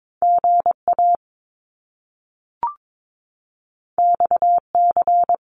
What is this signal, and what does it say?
Morse code, heard as a keyed tone, about 22 wpm, 700 Hz. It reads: GA XC
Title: Top Two Letter Combinations in Words 22wpm